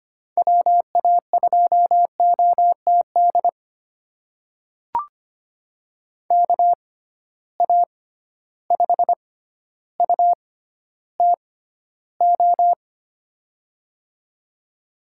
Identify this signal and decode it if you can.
Morse code, heard as a keyed tone, about 25 wpm, spaced as for 13 wpm, 700 Hz. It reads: WA2OTD KA5UTO
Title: USA Call Signs - 6 Characters - 6x Character Spacing - 25wpm